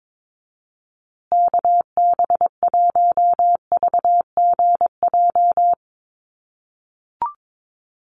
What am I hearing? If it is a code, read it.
Morse code, heard as a keyed tone, about 22 wpm, 700 Hz. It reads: KB1VGJ
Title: USA Call Signs Repeated 2x 22wpm